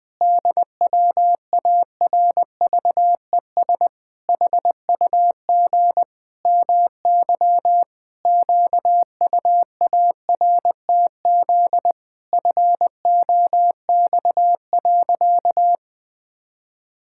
Morse code, heard as a keyed tone, about 20 wpm, 700 Hz. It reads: DWARVES HUG MY QUARTZ FOX.
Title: Pangram Challenge v2 - 20wpm